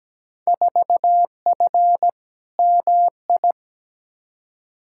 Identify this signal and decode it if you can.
Morse code, heard as a keyed tone, about 17 wpm, 700 Hz. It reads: 4F MI
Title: Field Day Exchanges Encore - 17wpm